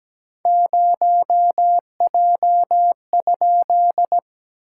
Morse code, heard as a keyed tone, about 17 wpm, 700 Hz. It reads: 0J?
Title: SOTA QSO Practice 17wpm